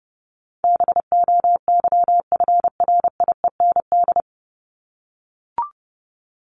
Morse code, heard as a keyed tone, about 30 wpm, 700 Hz. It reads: BOYFRIEND
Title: Top 501-750 Words 30wpm